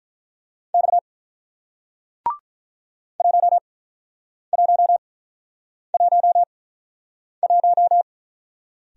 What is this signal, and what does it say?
Morse code, heard as a keyed tone, about 48 wpm, spaced as for 9 wpm, 700 Hz. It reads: X 1111